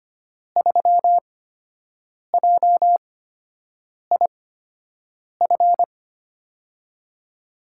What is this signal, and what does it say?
Morse code, heard as a keyed tone, about 25 wpm, spaced as for 7 wpm, 700 Hz. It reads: 3JIF
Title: USA Call Signs - 6 Characters - 8x Character Spacing - 25wpm